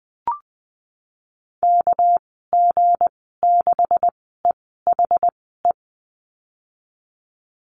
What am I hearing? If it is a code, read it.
Morse code, heard as a keyed tone, about 20 wpm, spaced as for 15 wpm, 700 Hz. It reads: KG6EHE
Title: 207 Lesson 33g - Letter j - Call Signs 20wpm